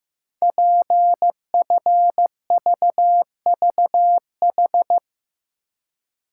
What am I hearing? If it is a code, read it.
Morse code, heard as a keyed tone, about 15 wpm, 700 Hz. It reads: PFVVH
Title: Code Groups - 5 Letters - 15wpm